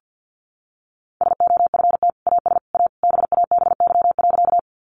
Morse code, heard as a keyed tone, about 50 wpm, 700 Hz. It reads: SOFT AS A BABY.
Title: Sentences from Top 1500 Words 50wpm